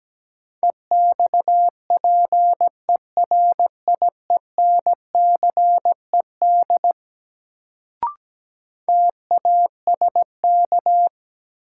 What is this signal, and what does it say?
Morse code, heard as a keyed tone, about 17 wpm, 700 Hz. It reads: EXPERIENCED TASK
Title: Sentences from Top 1100 Words - New Word Drill Alt - 17wpm